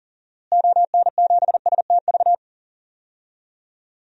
Morse code, heard as a keyed tone, about 40 wpm, 700 Hz. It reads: ON7STV
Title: DXCC-100 Call Signs - 6 Characters - Repeated 2x - 40wpm